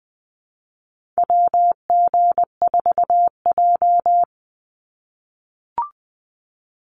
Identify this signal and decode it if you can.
Morse code, heard as a keyed tone, about 20 wpm, 700 Hz. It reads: WG4J
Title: USA Call Signs 20wpm